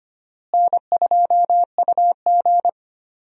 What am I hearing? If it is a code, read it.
Morse code, heard as a keyed tone, about 25 wpm, 700 Hz. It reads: N2UG
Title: USA Call Signs - 4 Characters - Repeated 2x - 25wpm